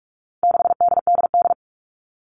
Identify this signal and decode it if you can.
Morse code, heard as a keyed tone, about 44 wpm, 700 Hz. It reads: 6DDD
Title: All Cut Numbers - 4-Digit 45wpm